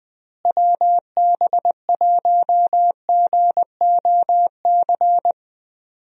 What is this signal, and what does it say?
Morse code, heard as a keyed tone, about 20 wpm, 700 Hz. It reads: WB1GOC
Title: USA Call Signs - 6 Characters - Repeated 3x - 20wpm